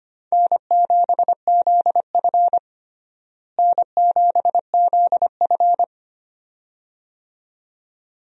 Morse code, heard as a keyed tone, about 25 wpm, 700 Hz. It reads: N7ZF N7ZF
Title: USA Call Signs - 4 Characters - Repeated 2x - 25wpm